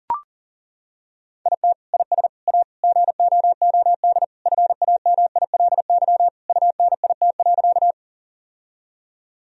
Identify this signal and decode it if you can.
Morse code, heard as a keyed tone, about 40 wpm, 700 Hz. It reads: IT IS A GOOD FAMILY UNIT.
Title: Sentences from Top 1000 Words - Review Part 2 - 40wpm